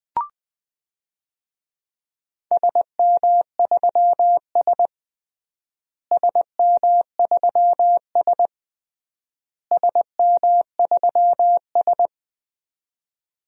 Morse code, heard as a keyed tone, about 20 wpm, 700 Hz. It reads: SM3S SM3S SM3S
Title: DXCC-100 Call Signs - 4 Characters - Repeated 3x - 20wpm